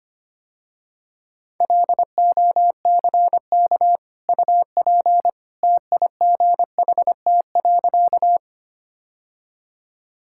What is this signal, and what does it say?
Morse code, heard as a keyed tone, about 25 wpm, 700 Hz. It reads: LOCK UP TIGHT.
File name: Easy Sentences with Hints - 25wpm